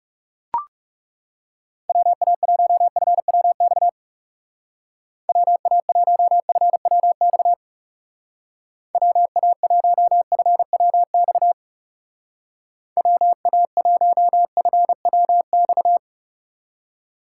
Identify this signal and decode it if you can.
Morse code, heard as a keyed tone, about 38 wpm, 700 Hz. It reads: WA1FWX WA1FWX WA1FWX WA1FWX
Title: Speed Racing - USA 2x3 Call Signs 30wpm